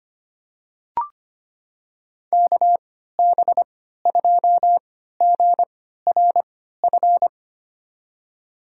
Morse code, heard as a keyed tone, about 25 wpm, spaced as for 14 wpm, 700 Hz. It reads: KB2GRF
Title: USA Call Signs - 3x Character Spacing - 25wpm